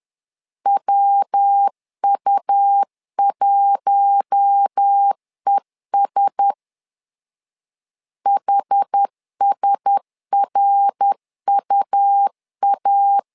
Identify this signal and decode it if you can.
Morse code, heard as a keyed tone, about 10 wpm, 800 Hz. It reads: WU1ES HSRUA
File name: LEZIONI / 4 Q1S 10wpm 2133